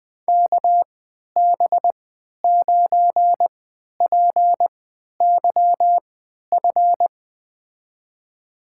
Morse code, heard as a keyed tone, about 21 wpm, spaced as for 11 wpm, 700 Hz. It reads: KB9PYF
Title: USA Call Signs - 6 Characters - 3x Character Spacing - 20wpm